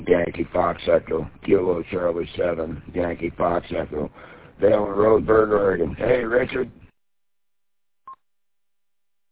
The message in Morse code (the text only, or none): none